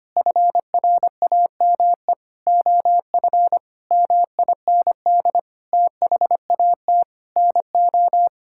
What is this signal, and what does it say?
Morse code, heard as a keyed tone, about 25 wpm, 700 Hz. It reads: FRAME OF MIND THAT NO
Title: Sentences from Top 1000 Words - Review Part 6 - 25wpm